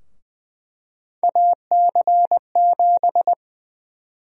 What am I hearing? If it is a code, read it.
Morse code, heard as a keyed tone, about 20 wpm, 700 Hz. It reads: AC7
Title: USA Call Sign Prefixes - 20wpm